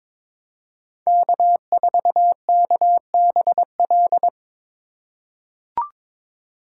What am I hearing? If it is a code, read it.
Morse code, heard as a keyed tone, about 22 wpm, 700 Hz. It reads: K4KBL
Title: K1USN SST QSO Practice - 22wpm